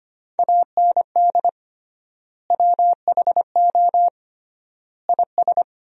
Morse code, heard as a keyed tone, about 25 wpm, 700 Hz. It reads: AND WHO IS
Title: 3x Word-Spacing - Sentences from Top 300 Words - 25wpm